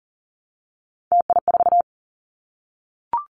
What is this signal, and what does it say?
Morse code, heard as a keyed tone, about 40 wpm, 700 Hz. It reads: TI4
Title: Callsign Prefixes for Top 100 DXCC - 40wpm